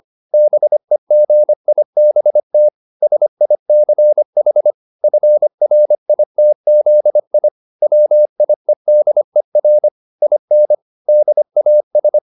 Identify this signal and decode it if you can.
Morse code, heard as a keyed tone, about 25 wpm, 600 Hz. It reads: BEGIBT SICH FRITZI WIEDER IN DAS